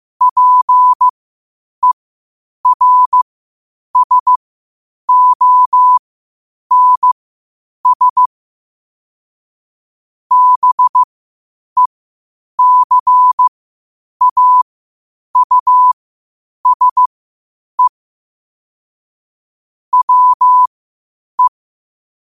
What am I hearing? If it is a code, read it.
Morse code, heard as a keyed tone, about 15 wpm, spaced as for 8 wpm, 1000 Hz. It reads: PERSONS BECAUSE WE
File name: qotdfiles / 8wpm